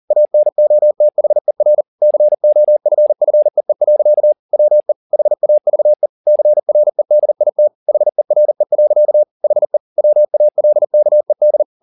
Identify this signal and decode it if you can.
Morse code, heard as a keyed tone, about 40 wpm, 600 Hz. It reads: ANOTHER COFFEE. WE HAVE CREDIT HERE. HE WALKED